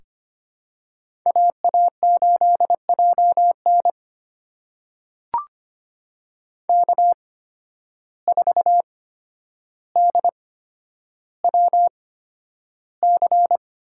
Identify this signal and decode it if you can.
Morse code, heard as a keyed tone, about 25 wpm, spaced as for 11 wpm, 700 Hz. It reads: AA8JN K4DWC
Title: USA Call Signs - 5 Characters - 8x Character Spacing - 25wpm